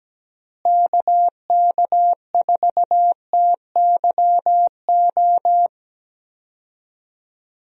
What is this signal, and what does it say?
Morse code, heard as a keyed tone, about 17 wpm, 700 Hz. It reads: KK4TYO
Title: USA Call Signs - 6 Characters - Repeated 3x - 17wpm